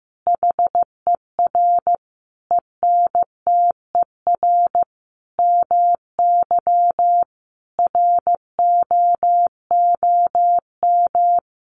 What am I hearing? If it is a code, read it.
Morse code, heard as a keyed tone, about 15 wpm, 700 Hz. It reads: HER ENTER MY ROOM
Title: Sentences from Top 1300 Words 15wpm